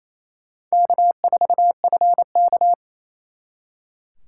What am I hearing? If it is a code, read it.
Morse code, heard as a keyed tone, about 32 wpm, 700 Hz. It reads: K4FK
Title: Speed Racing - USA 1x2 Call Signs 28wpm